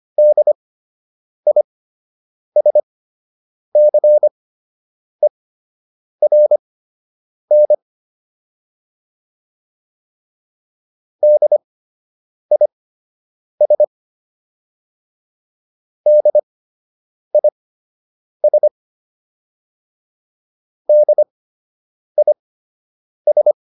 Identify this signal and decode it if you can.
Morse code, heard as a keyed tone, about 25 wpm, spaced as for 8 wpm, 600 Hz. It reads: DISCERN DIS DIS DIS